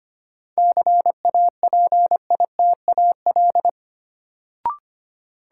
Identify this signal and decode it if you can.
Morse code, heard as a keyed tone, about 25 wpm, 700 Hz. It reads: CAPITAL
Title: Sentences from Top 1000 Words - New Word Drill - 25wpm